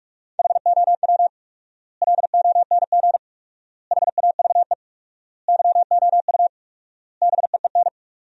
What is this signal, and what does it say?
Morse code, heard as a keyed tone, about 46 wpm, 700 Hz. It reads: HOW LONG HAVE YOU BEEN
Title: 4x Word-Spacing - Sentences from Top 200 Words - 45wpm